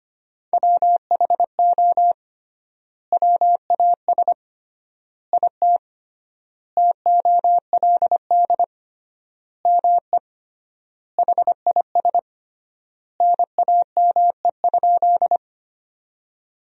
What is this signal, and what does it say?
Morse code, heard as a keyed tone, about 25 wpm, 700 Hz. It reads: WHO WAS IT TOLD ME HIS NAME?
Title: 3x Word-Spacing - Sentences from Top 400 Words - 25wpm